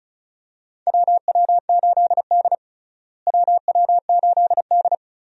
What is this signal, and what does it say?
Morse code, heard as a keyed tone, about 36 wpm, 700 Hz. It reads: WW8D WW8D
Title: USA Call Signs - 4 Characters - Repeated 2x - 35wpm